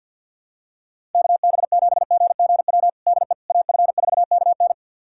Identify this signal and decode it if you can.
Morse code, heard as a keyed tone, about 52 wpm, 700 Hz. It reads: KB7GGW DE AF4KN
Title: USA Callsign DE Callsign 50wpm